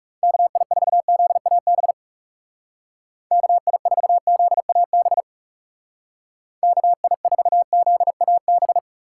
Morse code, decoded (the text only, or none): KI4ZAB KI4ZAB KI4ZAB